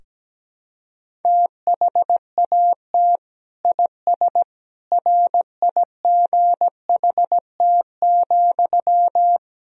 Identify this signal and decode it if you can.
Morse code, heard as a keyed tone, about 17 wpm, 700 Hz. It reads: THAT IS RIGHT,
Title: Sentences from Top 200 Words and Abbreviations 17wpm